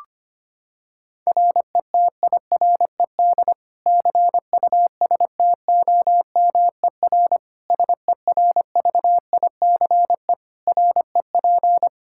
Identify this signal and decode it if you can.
Morse code, heard as a keyed tone, about 25 wpm, 700 Hz. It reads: RETIRED CUSTOMER SERVICE REP
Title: Practice QSO - Exchange 3 - More Info - 25wpm